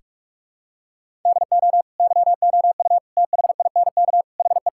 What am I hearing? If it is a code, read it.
Morse code, heard as a keyed tone, about 46 wpm, 700 Hz. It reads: DO YOU THINK HE
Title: Sentences from Top 1600 Words 45wpm